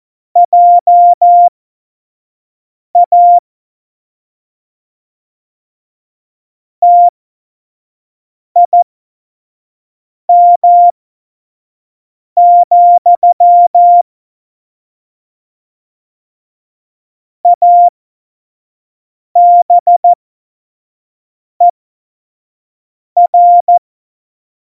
Morse code, heard as a keyed tone, about 14 wpm, spaced as for 5 wpm, 700 Hz. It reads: JA TIM, ABER
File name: Morsekurs / Witze / Witz 28. November 2017, 25 F70